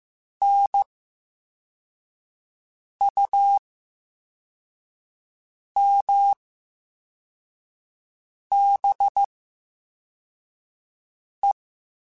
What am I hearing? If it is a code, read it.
Morse code, heard as a keyed tone, about 15 wpm, spaced as for 4 wpm, 800 Hz. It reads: NUMBE
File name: Special INGs / Special INGs-mp3s / Slow ing @13